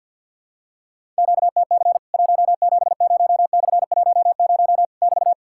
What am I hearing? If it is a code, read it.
Morse code, heard as a keyed tone, about 52 wpm, 700 Hz. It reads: QTC 170/10 <BT>
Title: Worked All Europe DX Contest - QTC Exchange with Repeats - 50wpm